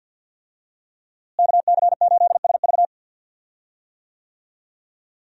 Morse code, heard as a keyed tone, about 46 wpm, 700 Hz. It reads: KC8SV
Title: USA Call Signs - 5 Characters - Repeated 3x - 50wpm